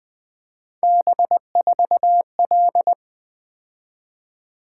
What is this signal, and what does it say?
Morse code, heard as a keyed tone, about 20 wpm, 700 Hz. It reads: B4L
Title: Three Letter-Number Combinations - 20wpm